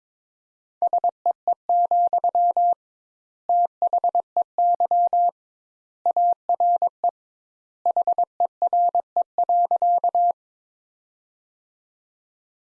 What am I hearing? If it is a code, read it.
Morse code, heard as a keyed tone, about 22 wpm, 700 Hz. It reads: SEE, THEY ARE HERE.
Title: Sentences Top 200 Words - 22wpm